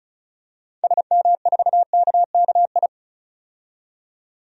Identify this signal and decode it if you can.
Morse code, heard as a keyed tone, about 36 wpm, 700 Hz. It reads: SM4KKI
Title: DXCC-100 Call Signs - 6 Characters - 35wpm